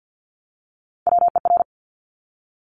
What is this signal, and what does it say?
Morse code, heard as a keyed tone, about 50 wpm, 700 Hz. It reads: WER
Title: Top Three Letter Combinations in Words 50wpm